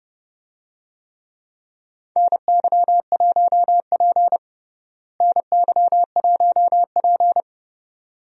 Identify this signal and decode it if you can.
Morse code, heard as a keyed tone, about 32 wpm, 700 Hz. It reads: NY1P NY1P
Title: USA Call Signs - 4 Characters - Repeated 2x - 30wpm